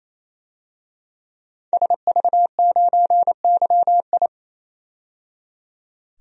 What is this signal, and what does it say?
Morse code, heard as a keyed tone, about 28 wpm, 700 Hz. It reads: SV9YI
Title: DXCC-100 Call Signs - 5 Characters - 28wpm